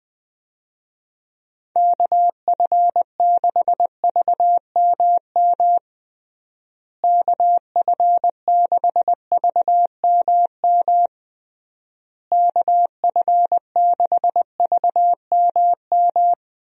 Morse code, heard as a keyed tone, about 21 wpm, 700 Hz. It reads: KF6VMM KF6VMM KF6VMM
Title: USA Call Signs - 6 Characters - Repeated 3x - 20wpm